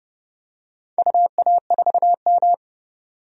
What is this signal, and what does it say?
Morse code, heard as a keyed tone, about 29 wpm, 700 Hz. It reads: UA4M
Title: DXCC-100 Call Signs - 4 Characters - Repeated 2x - 30wpm